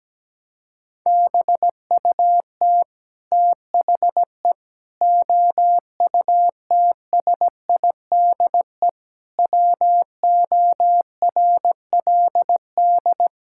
Morse code, heard as a keyed tone, about 17 wpm, 700 Hz. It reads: BUT THE OUTSIDE WORLD
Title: Sentences from Top 1000 Words - Review Part 6 - 17wpm